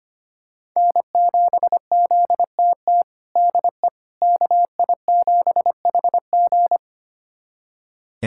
Morse code, heard as a keyed tone, about 25 wpm, 700 Hz. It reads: N7ZTT DE KI7HG
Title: USA Callsign DE Callsign - 5 Characters - 25wpm